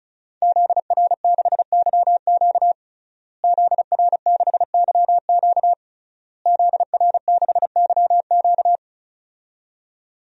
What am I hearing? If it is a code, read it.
Morse code, heard as a keyed tone, about 36 wpm, 700 Hz. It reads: ZR6YQ ZR6YQ ZR6YQ
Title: DXCC-100 Call Signs - 5 Characters - Repeated 3x - 35wpm